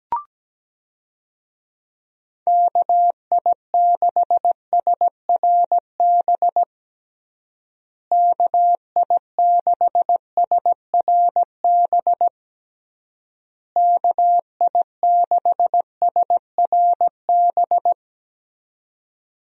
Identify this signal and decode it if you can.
Morse code, heard as a keyed tone, about 17 wpm, 700 Hz. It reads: KI6SRB KI6SRB KI6SRB
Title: USA Call Signs - 6 Characters - Repeated 3x - 17wpm